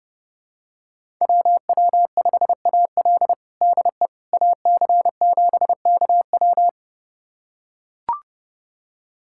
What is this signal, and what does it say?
Morse code, heard as a keyed tone, about 30 wpm, 700 Hz. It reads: WW5AL DE AC7KW
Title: USA Callsign DE Callsign - 5 Characters - 30wpm